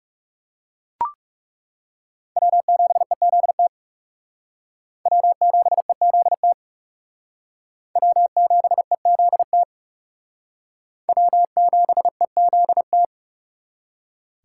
Morse code, heard as a keyed tone, about 36 wpm, 700 Hz. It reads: W7EZT W7EZT W7EZT W7EZT
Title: Speed Racing - USA Call Signs 30wpm